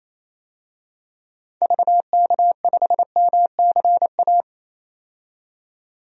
Morse code, heard as a keyed tone, about 27 wpm, 700 Hz. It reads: VK5MCA
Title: DXCC-100 Call Signs - 6 Characters - 28wpm